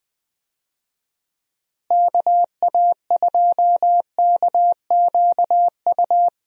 Morse code, heard as a keyed tone, about 21 wpm, 700 Hz. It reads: KA2KQU
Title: USA Call Signs - 6 Characters - Repeated 2x - 20wpm